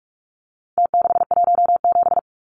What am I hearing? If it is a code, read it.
Morse code, heard as a keyed tone, about 44 wpm, 700 Hz. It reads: T617
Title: Numbers - Contest Style - 4-Digit 45wpm